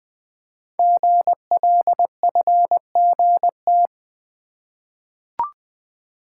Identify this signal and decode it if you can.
Morse code, heard as a keyed tone, about 20 wpm, 700 Hz. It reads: GLFGT
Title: Code Groups - 5 Letters - 20wpm